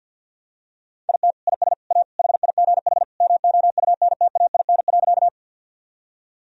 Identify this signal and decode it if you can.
Morse code, heard as a keyed tone, about 49 wpm, 700 Hz. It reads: IT IS A HIGH MOUNTAIN.
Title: Sentences from Top 1000 Words - Review Part 1 - 50wpm